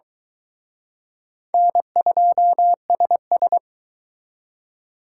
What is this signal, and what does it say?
Morse code, heard as a keyed tone, about 24 wpm, 700 Hz. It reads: N2SS